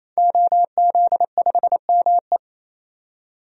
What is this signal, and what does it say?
Morse code, heard as a keyed tone, about 28 wpm, 700 Hz. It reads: OZ5ME
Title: DXCC-100 Call Signs - 5 Characters - 28wpm